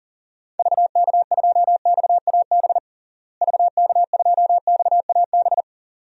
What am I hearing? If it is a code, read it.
Morse code, heard as a keyed tone, about 40 wpm, 700 Hz. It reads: VK2XAB VK2XAB
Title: DXCC-100 Call Signs - 6 Characters - Repeated 2x - 40wpm